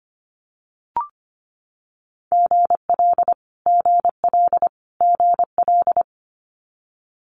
Morse code, heard as a keyed tone, about 25 wpm, 700 Hz. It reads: GL GL GL
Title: Multi-letter Phonemes Repeated 3x 25wpm